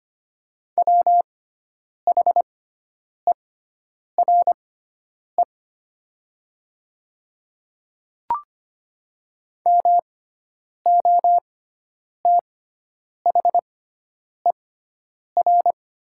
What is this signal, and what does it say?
Morse code, heard as a keyed tone, about 25 wpm, spaced as for 9 wpm, 700 Hz. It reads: WHERE MOTHER
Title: 6x Character Spacing - Top 200 Words - 25wpm